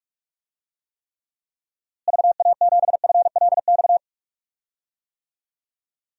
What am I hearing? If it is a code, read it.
Morse code, heard as a keyed tone, about 43 wpm, 700 Hz. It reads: VA7FLX
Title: DXCC-100 Call Signs - 6 Characters - Repeated 2x - 45wpm